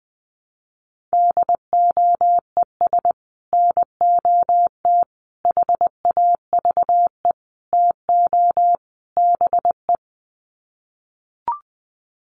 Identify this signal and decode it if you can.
Morse code, heard as a keyed tone, about 20 wpm, 700 Hz. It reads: DOES NOT HAVE TO BE
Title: Sets of 5 Words 20wpm